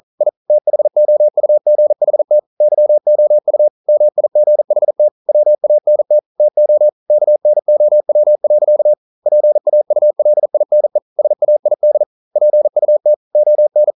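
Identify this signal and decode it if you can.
Morse code, heard as a keyed tone, about 42 wpm, 600 Hz. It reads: I THOUGHT YOU MIGHT WANT TO KNOW. PAULINE SAID PUT ON
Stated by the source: Morse code practice at 41 wpm, from CW med Gnister.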